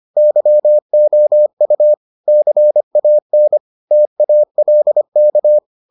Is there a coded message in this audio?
Yes: Morse code from 0.2 to 5.6 s, heard as a keyed tone, about 25 wpm, 600 Hz. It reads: YOU CAN TALK